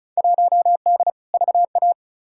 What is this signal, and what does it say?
Morse code, heard as a keyed tone, about 34 wpm, 700 Hz. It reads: 1D VA
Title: Field Day Exchanges - 35wpm